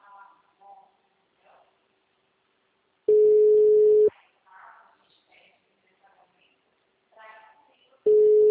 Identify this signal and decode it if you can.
Ringback tone (Europe, 425 Hz), with speech.